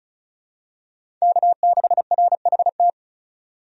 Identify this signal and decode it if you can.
Morse code, heard as a keyed tone, about 36 wpm, 700 Hz. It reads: K6RHT
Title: USA Call Signs - 5 Characters - 35wpm